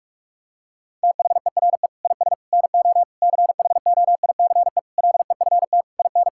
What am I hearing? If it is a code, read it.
Morse code, heard as a keyed tone, about 46 wpm, 700 Hz. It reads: THERE IS NO CHOICE LEFT IN
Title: Sentences from Top 1000 Words - Review Part 6 - 45wpm